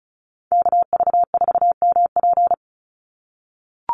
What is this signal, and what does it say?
Morse code, heard as a keyed tone, about 34 wpm, 700 Hz. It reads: KV4MP
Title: USA Call Signs 35wpm